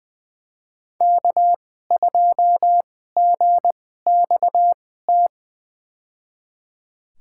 Morse code, heard as a keyed tone, about 20 wpm, spaced as for 14 wpm, 700 Hz. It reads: K2GXT
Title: USA Call Signs - 5 Characters - 2x Character Spacing - 20wpm